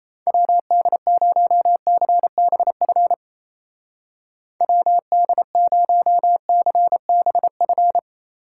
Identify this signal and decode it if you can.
Morse code, heard as a keyed tone, about 34 wpm, 700 Hz. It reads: WD0CBF WD0CBF
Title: Speed Racing - USA Call Signs 28wpm